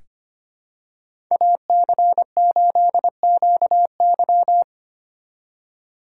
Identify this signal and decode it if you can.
Morse code, heard as a keyed tone, about 25 wpm, 700 Hz. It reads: AC8QY
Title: USA Call Signs - 5 Characters - 3x Character Spacing - 25wpm